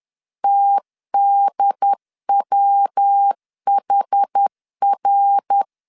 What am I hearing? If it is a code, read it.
Morse code, heard as a keyed tone, about 10 wpm, 800 Hz. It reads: TDWHR
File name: CW / DET / 1 3 1500C S C 10wpm 3335